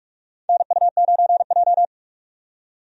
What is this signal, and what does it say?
Morse code, heard as a keyed tone, about 45 wpm, 700 Hz. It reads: NU9J
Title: USA Call Signs - 4 Characters - 45wpm